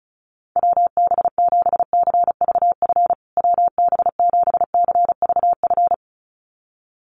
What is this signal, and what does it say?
Morse code, heard as a keyed tone, about 34 wpm, 700 Hz. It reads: WB7CVF WB7CVF
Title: USA Call Signs Repeated 2x 35wpm